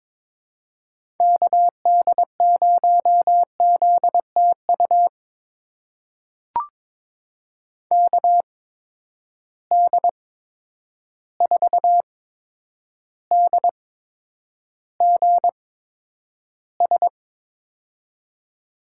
Morse code, heard as a keyed tone, about 22 wpm, spaced as for 9 wpm, 700 Hz. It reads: KD0ZTU KD4DGS
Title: USA Call Signs - 6 Characters - 8x Character Spacing - 22wpm